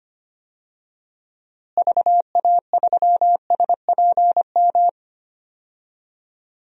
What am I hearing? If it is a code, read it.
Morse code, heard as a keyed tone, about 25 wpm, 700 Hz. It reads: VA3SPM
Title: DXCC-100 Call Signs - 6 Characters - Repeated 3x - 25wpm